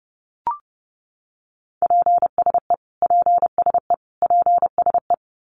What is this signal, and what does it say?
Morse code, heard as a keyed tone, about 30 wpm, 700 Hz. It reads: PSE PSE PSE
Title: Abbreviations Repeated 3x - 30wpm